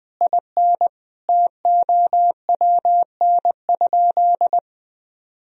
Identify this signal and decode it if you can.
Morse code, heard as a keyed tone, about 20 wpm, 700 Hz. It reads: IN TOWN?
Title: Sentences from Top 1000 Words - Review Part 2 - 20wpm